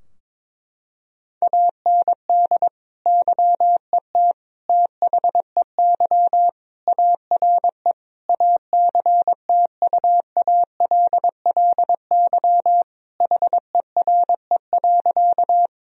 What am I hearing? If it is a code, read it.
Morse code, heard as a keyed tone, about 22 wpm, 700 Hz. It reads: AND YET THEY ARE ACTUALLY HERE.
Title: Sentences from Top 1000 Words - Review Part 4 - 22wpm